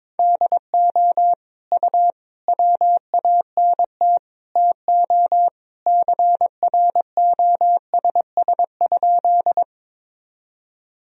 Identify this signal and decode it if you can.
Morse code, heard as a keyed tone, about 22 wpm, 700 Hz. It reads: DO U WANT TO CROSS?
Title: Sentences from Top 300 Words and Abbreviations 22wpm